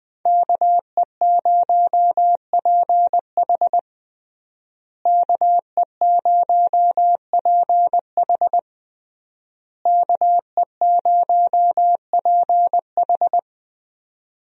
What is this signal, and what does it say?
Morse code, heard as a keyed tone, about 20 wpm, 700 Hz. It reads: KE0PH KE0PH KE0PH
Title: USA Call Signs - 5 Characters - Repeated 3x - 20wpm